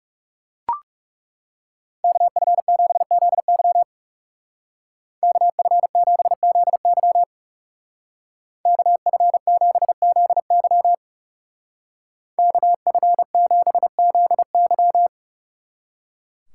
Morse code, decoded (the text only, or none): KF7ZY KF7ZY KF7ZY KF7ZY